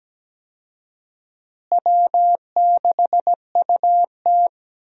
Morse code, heard as a keyed tone, about 17 wpm, 700 Hz. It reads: W6UT
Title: USA Call Signs - 4 Characters - Repeated 2x - 17wpm